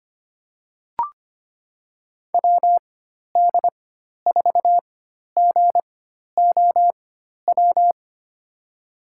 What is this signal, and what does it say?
Morse code, heard as a keyed tone, about 25 wpm, spaced as for 12 wpm, 700 Hz. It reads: WD4GOW
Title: USA Call Signs - 6 Characters - 4x Character Spacing - 25wpm